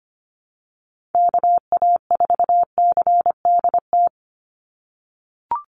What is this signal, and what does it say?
Morse code, heard as a keyed tone, about 25 wpm, 690 Hz. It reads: KA4CDT
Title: USA Call Signs 25wpm